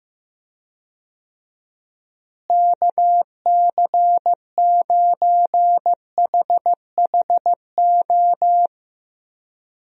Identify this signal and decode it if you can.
Morse code, heard as a keyed tone, about 15 wpm, 700 Hz. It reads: KC9HHO
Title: USA Call Signs - 6 Characters - Repeated 2x - 15wpm